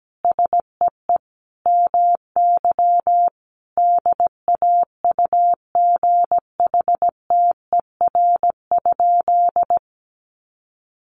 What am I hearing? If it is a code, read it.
Morse code, heard as a keyed tone, about 17 wpm, 700 Hz. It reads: SEE MY DAUGHTER?
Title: Sentences from Top 1500 Words 17wpm